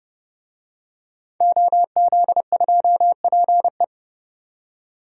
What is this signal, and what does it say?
Morse code, heard as a keyed tone, about 30 wpm, 700 Hz. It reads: OZ2PE
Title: DXCC-100 Call Signs - 5 Characters - 30wpm